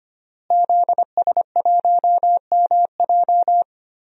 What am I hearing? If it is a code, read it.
Morse code, heard as a keyed tone, about 25 wpm, 700 Hz. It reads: ZS1MJ
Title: DXCC-100 Call Signs - 5 Characters - Repeated 2x - 25wpm